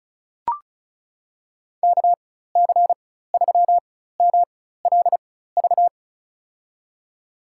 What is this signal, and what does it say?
Morse code, heard as a keyed tone, about 36 wpm, spaced as for 16 wpm, 700 Hz. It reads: KC3MLV
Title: USA Call Signs - 4x Character Spacing - 35wpm